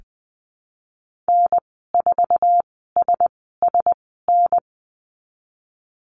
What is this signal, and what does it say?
Morse code, heard as a keyed tone, about 20 wpm, spaced as for 15 wpm, 700 Hz. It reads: N4SSN
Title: 062 Lesson 11g - Letter h - Call Signs 20wpm